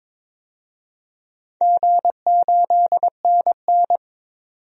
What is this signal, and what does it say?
Morse code, heard as a keyed tone, about 22 wpm, 700 Hz. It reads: G8NN
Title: DXCC-100 Call Signs - 4 Characters - 22wpm